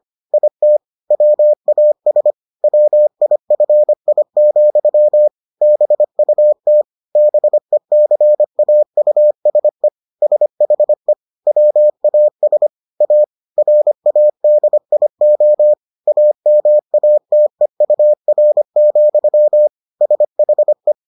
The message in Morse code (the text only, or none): IT WAS WIFI, BUT BECAUSE SHE WAS A RADIO AMATEUR, SHE